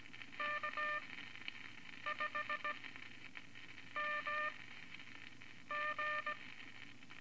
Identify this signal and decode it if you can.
Morse code, heard as a keyed tone, about 17 wpm, spaced as for 6 wpm, 1200 Hz, with noise. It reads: K5MG
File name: Morse Code Records / 13A LCM 154 Units I-IV Groups of Four cont